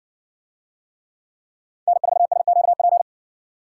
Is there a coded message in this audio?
Yes: Morse code from 1.9 to 3.0 s, heard as a keyed tone, about 57 wpm, 700 Hz. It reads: N4SQP